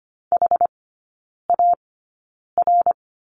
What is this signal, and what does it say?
Morse code, heard as a keyed tone, about 25 wpm, spaced as for 9 wpm, 690 Hz. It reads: HAR
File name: Lesson 14e - Number 2 - Three Characters 25wpm